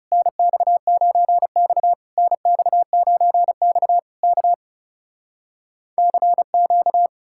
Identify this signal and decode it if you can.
Morse code, heard as a keyed tone, about 40 wpm, 700 Hz. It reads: NX9X NX9X K CQ